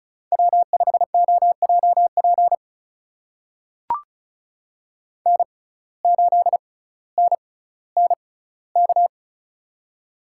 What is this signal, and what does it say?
Morse code, heard as a keyed tone, about 36 wpm, spaced as for 18 wpm, 700 Hz. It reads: W5OJP N8NNK
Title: USA Call Signs - 5 Characters - 6x Character Spacing - 35wpm